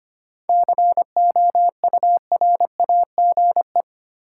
Morse code, heard as a keyed tone, about 25 wpm, 700 Hz. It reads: COURAGE
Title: Seven Letter Words - 25wpm